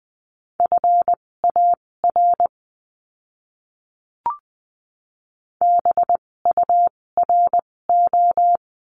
Morse code, heard as a keyed tone, about 20 wpm, spaced as for 16 wpm, 700 Hz. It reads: FAR BURO
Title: 256 Lesson 42a - Words 12-20wpm